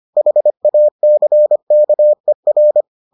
Morse code, heard as a keyed tone, about 25 wpm, 600 Hz. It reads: HACKER